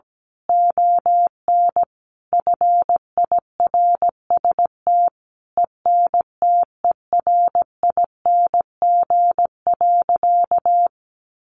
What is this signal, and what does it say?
Morse code, heard as a keyed tone, about 17 wpm, 700 Hz. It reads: ON FIRST ENTERING.
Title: A Sound Of Thunder 17wpm